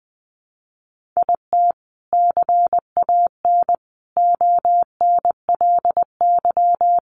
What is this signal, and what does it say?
Morse code, heard as a keyed tone, about 20 wpm, 700 Hz. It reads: IT CAN ONLY
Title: Sentences from Top 1400 Words 20wpm